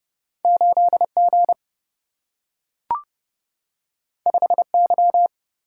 Morse code, heard as a keyed tone, about 30 wpm, 700 Hz. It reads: 8G 5Y
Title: Single-Letter Single-Number 30wpm